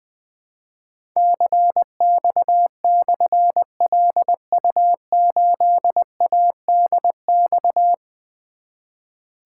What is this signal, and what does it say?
Morse code, heard as a keyed tone, about 20 wpm, 700 Hz. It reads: CX/LU8ADX
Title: Difficult Call Signs 20wpm